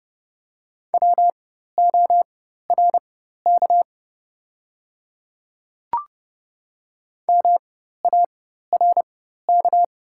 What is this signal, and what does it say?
Morse code, heard as a keyed tone, about 29 wpm, spaced as for 14 wpm, 700 Hz. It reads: WORK MARK
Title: 4x Character Spacing - Top 300 Words - 30wpm